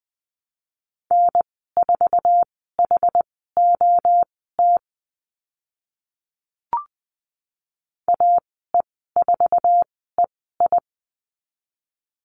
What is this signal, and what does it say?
Morse code, heard as a keyed tone, about 20 wpm, spaced as for 14 wpm, 700 Hz. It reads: N4HOT AE4EI
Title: 076 Lesson 13g - Letter L - Call Signs 20wpm